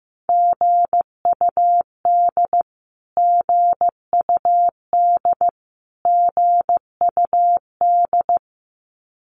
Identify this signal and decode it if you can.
Morse code, heard as a keyed tone, about 15 wpm, 700 Hz. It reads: GUD GUD GUD
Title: Abbreviations Repeated 3x - 15wpm